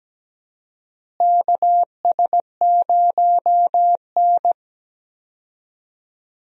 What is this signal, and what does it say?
Morse code, heard as a keyed tone, about 17 wpm, 700 Hz. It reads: KS0N
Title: USA Call Signs - 4 Characters - Repeated 2x - 17wpm